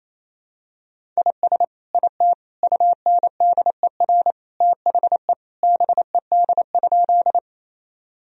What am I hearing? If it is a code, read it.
Morse code, heard as a keyed tone, about 28 wpm, 700 Hz. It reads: IS IT UNDER THE BED?
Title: Sentences from Top 1000 Words - Review Part 1 - 28wpm